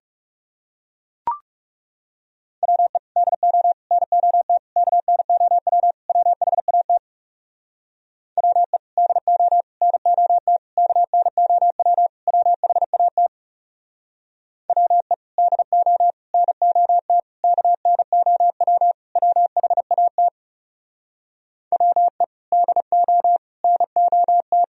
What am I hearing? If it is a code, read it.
Morse code, heard as a keyed tone, about 38 wpm, 700 Hz. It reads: WE DO NOT KNOW WHAT WE DO NOT KNOW WHAT WE DO NOT KNOW WHAT WE DO NOT